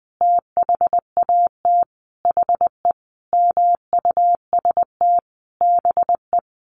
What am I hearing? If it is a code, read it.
Morse code, heard as a keyed tone, about 19 wpm, 700 Hz. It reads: THAT HE MUST BE
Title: Sentences from Top 1500 Words 20wpm